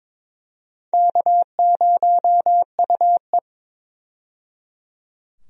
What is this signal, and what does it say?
Morse code, heard as a keyed tone, about 22 wpm, 700 Hz. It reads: K0UE
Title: USA Call Signs - 4 Characters - 22wpm